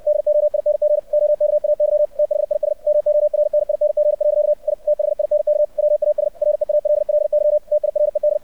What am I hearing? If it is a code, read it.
Morse code, heard as a keyed tone, about 64 wpm, 600 Hz. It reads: DOING OKAY THIS MORNING. I THINK WAS REALLY TIRED
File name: archived / CRSnet